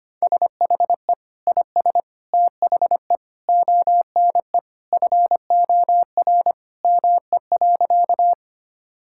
Morse code, heard as a keyed tone, about 25 wpm, 700 Hz. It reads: SHE IS THE ONE FOR ME.
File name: practice / 25wpm / Sentences with 2-Word Phrases - 25wpm